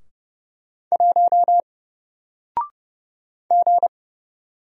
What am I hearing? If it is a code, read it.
Morse code, heard as a keyed tone, about 30 wpm, 700 Hz. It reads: 1 G
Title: Single Character (Letter:Number:Punctuation:Prosign) Alt 30wpm